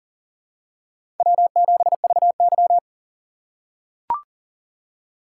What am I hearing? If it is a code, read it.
Morse code, heard as a keyed tone, about 40 wpm, 700 Hz. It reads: W7VY
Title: USA Call Signs - 4 Characters - 3x Character Spacing - 40wpm